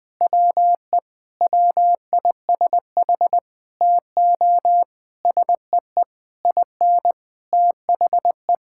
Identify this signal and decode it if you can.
Morse code, heard as a keyed tone, about 21 wpm, 700 Hz. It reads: WE WISH TO SEE IN THE
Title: Sentences with 9-Word Phrases - Spoken Beginning - 20wpm